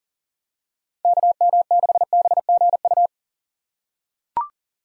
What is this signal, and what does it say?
Morse code, heard as a keyed tone, about 40 wpm, 700 Hz. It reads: KM6BGU
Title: USA Call Signs - 6x Character Spacing - 40wpm